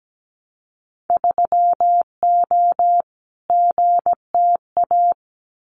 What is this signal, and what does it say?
Morse code, heard as a keyed tone, about 17 wpm, 700 Hz. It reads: 3O GTA
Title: Winter Field Day Exchanges Encore 17wpm